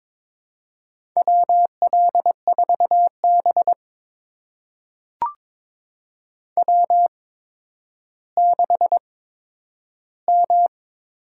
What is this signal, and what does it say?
Morse code, heard as a keyed tone, about 23 wpm, spaced as for 11 wpm, 700 Hz. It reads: WL4B W6M
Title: USA Call Signs - 4 Characters - 8x Character Spacing - 22wpm